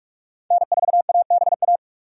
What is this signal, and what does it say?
Morse code, heard as a keyed tone, about 46 wpm, 700 Hz. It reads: N4ABA
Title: USA Call Signs - 5 Characters - 45wpm